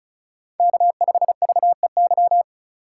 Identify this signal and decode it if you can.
Morse code, heard as a keyed tone, about 36 wpm, 700 Hz. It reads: K5VEY
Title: USA Call Signs - 3x Character Spacing - 35wpm